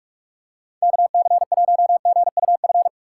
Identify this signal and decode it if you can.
Morse code, heard as a keyed tone, about 46 wpm, 700 Hz. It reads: KC1GUF
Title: USA Call Signs - 6 Characters - 3x Character Spacing - 45wpm